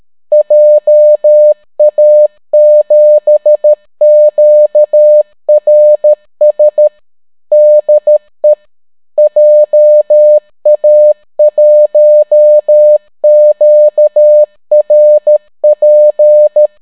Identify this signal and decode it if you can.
Morse code, heard as a keyed tone, about 13 wpm, 600 Hz. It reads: JA7QRS DE JA1QRP